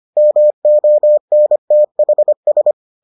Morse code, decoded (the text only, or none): MONTHS